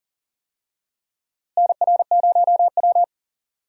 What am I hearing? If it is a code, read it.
Morse code, heard as a keyed tone, about 40 wpm, 700 Hz. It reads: NR0W
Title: USA Call Signs - 4 Characters - Repeated 2x - 40wpm